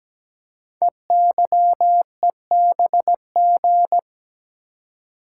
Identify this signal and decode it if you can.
Morse code, heard as a keyed tone, about 17 wpm, 700 Hz. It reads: EYEBG
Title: Code Groups - 5 Letters - 17wpm